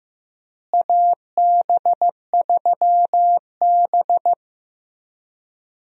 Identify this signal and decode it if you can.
Morse code, heard as a keyed tone, about 16 wpm, 700 Hz. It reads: AB3B
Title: USA Call Signs - 4 Characters - Repeated 3x - 15wpm